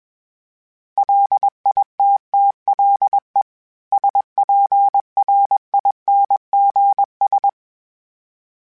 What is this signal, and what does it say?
Morse code, heard as a keyed tone, about 21 wpm, 800 Hz. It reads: LITTLE SPRINGS
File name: fromSMs / FromSMs mp3s / Cities@20